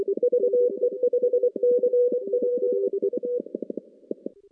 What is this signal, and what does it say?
Morse code, heard as a keyed tone, about 25 wpm, 490 Hz. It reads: VE5CPU
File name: contests / CW-SS-k5tr- -0000